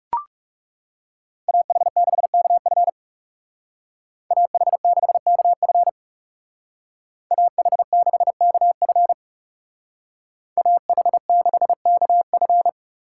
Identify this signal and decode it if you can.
Morse code, heard as a keyed tone, about 35 wpm, 700 Hz. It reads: AH6KF AH6KF AH6KF AH6KF
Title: Speed Racing - USA 2x2 Call Signs 30wpm